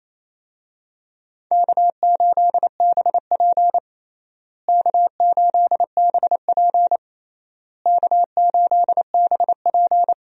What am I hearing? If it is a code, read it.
Morse code, heard as a keyed tone, about 28 wpm, 700 Hz. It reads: K8BP K8BP K8BP
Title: USA Call Signs - 4 Characters - Repeated 3x - 28wpm